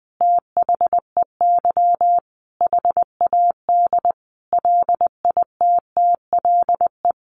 Morse code, heard as a keyed tone, about 20 wpm, 700 Hz. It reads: THEY HAD LITTLE
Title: Sentences from Top 1300 Words 20wpm